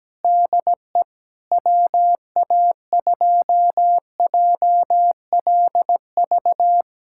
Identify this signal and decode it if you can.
Morse code, heard as a keyed tone, about 17 wpm, 700 Hz. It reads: DE WA2JLV
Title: USA Callsign DE Callsign - 6 Characters - 17wpm